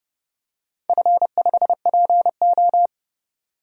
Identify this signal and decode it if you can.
Morse code, heard as a keyed tone, about 30 wpm, 700 Hz. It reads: F5PO
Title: DXCC-100 Call Signs - 4 Characters - 30wpm